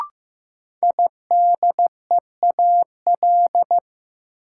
Morse code, heard as a keyed tone, about 15 wpm, 700 Hz. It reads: IDEAL
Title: Sentences from Top 1400 Words - New Word Drill - 15wpm